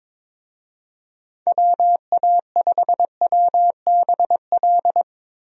Morse code, heard as a keyed tone, about 22 wpm, 700 Hz. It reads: WA5WBL
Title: USA Call Signs - 6 Characters - Repeated 3x - 22wpm